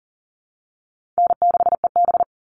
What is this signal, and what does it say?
Morse code, heard as a keyed tone, about 40 wpm, 700 Hz. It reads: N6EB
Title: All Cut Numbers - 4-Digit 40wpm